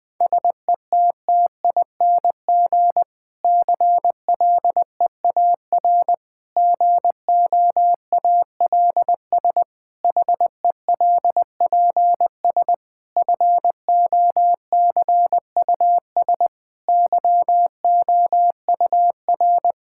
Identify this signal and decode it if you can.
Morse code, heard as a keyed tone, about 20 wpm, 700 Hz. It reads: SETTING CLEAR GOALS HELPS FOCUS YOUR
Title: Difficult Sentences with Hints - 20wpm